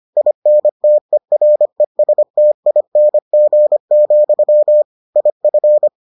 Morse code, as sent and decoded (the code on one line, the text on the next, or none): .. -. - . .-. ... - .. -. --. --..-- / .. ..-.
INTERESTING, IF